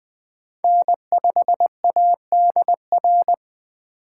Morse code, heard as a keyed tone, about 20 wpm, 700 Hz. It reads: N5ADR
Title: USA Call Signs - 5 Characters - 2x Character Spacing - 20wpm